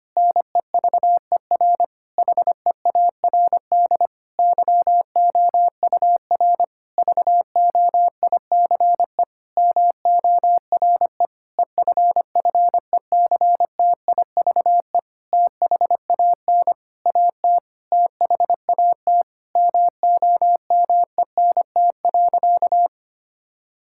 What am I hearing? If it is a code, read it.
Morse code, heard as a keyed tone, about 25 wpm, 700 Hz. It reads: NEVER HEARD YOUR VOICE MORE EFFECTIVE THAN AT THAT MOMENT.
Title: Sentences from Top 1000 Words - Review Part 6 - 25wpm